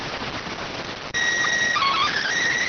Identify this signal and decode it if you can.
SSTV (slow-scan television), PD 290.